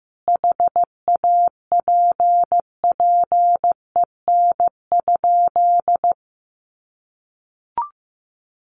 Morse code, heard as a keyed tone, about 15 wpm, 700 Hz. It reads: HAPPEN?
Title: Sentences from Top 1100 Words 15wpm